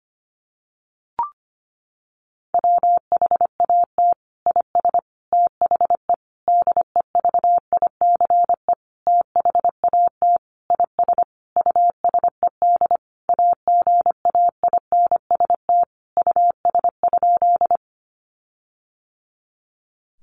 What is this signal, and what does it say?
Morse code, heard as a keyed tone, about 25 wpm, 700 Hz. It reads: WHAT IS THE DEVICE THAT IS USED AGAINST US?
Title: Sentences from Top 1500 Words 25wpm